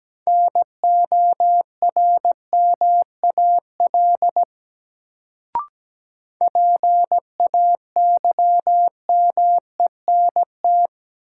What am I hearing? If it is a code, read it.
Morse code, heard as a keyed tone, about 17 wpm, 700 Hz. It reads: NORMAL PAYMENT
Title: Sentences from Top 1200 Words - New Word Drill - 17wpm